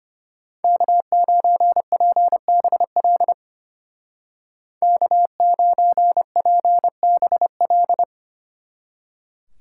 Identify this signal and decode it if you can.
Morse code, heard as a keyed tone, about 31 wpm, 700 Hz. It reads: K9PBL K9PBL
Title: Speed Racing - USA 1x3 Call Signs 25wpm